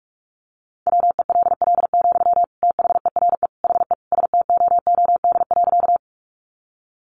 Morse code, heard as a keyed tone, about 46 wpm, 700 Hz. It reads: WELL, THERE HE STOOD.
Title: Sentences from Top 500 Words 45wpm